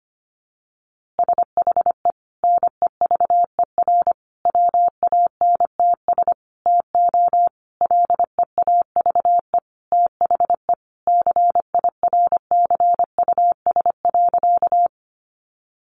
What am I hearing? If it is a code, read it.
Morse code, heard as a keyed tone, about 25 wpm, 700 Hz. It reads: SHE NEVER WANTS TO LEAVE THE CIRCUS.
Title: Easy Phrases 25wpm